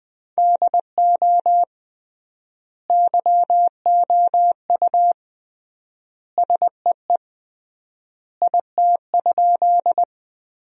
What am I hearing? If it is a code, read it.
Morse code, heard as a keyed tone, about 20 wpm, 700 Hz. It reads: DO YOU SEE IT?
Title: 3x Word-Spacing - Sentences from Top 100 Words - 20wpm